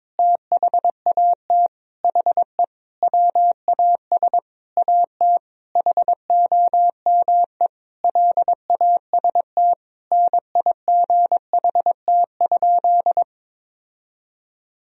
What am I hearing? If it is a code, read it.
Morse code, heard as a keyed tone, about 22 wpm, 700 Hz. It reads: THAT HE WAS AT HOME LAST NIGHT?
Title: Sentences from Top 1000 Words - Review Part 2 - 22wpm